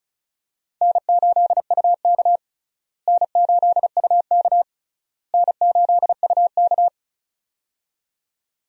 Morse code, heard as a keyed tone, about 36 wpm, 700 Hz. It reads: N8UK N8UK N8UK
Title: USA Call Signs - 4 Characters - Repeated 3x - 35wpm